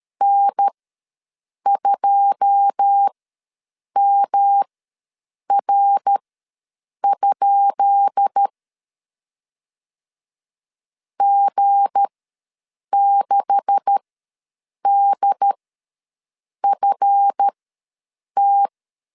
Morse code, heard as a keyed tone, about 13 wpm, spaced as for 7 wpm, 800 Hz. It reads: N2MR? G6DFT
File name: LEZIONI / 10 SIGNS / 10B INT ws 12wpm 3527